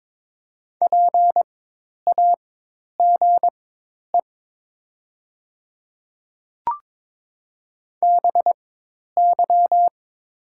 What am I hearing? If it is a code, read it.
Morse code, heard as a keyed tone, about 22 wpm, spaced as for 10 wpm, 700 Hz. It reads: PAGE BY
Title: 4x Character Spacing - Top 300 Words - 22wpm